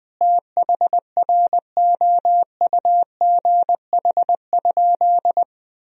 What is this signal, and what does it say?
Morse code, heard as a keyed tone, about 21 wpm, 700 Hz. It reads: THROUGH?
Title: Sentences from Top 1000 Words - Review Part 2 - 20wpm